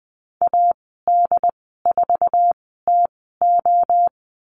Morse code, heard as a keyed tone, about 19 wpm, spaced as for 14 wpm, 700 Hz. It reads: AD4TO
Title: 088 Lesson 15g - Number 5 - Call Signs 20wpm